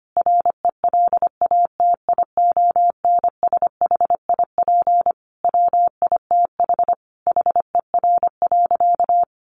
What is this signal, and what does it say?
Morse code, heard as a keyed tone, about 25 wpm, 700 Hz. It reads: RELATIONSHIP WITH HER.
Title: Sentences from Top 1100 Words 25wpm